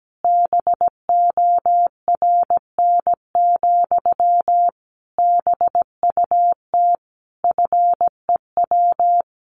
Morse code, heard as a keyed tone, about 17 wpm, 700 Hz. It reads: BORN, BUT FEW
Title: Sentences from Top 1300 Words 17wpm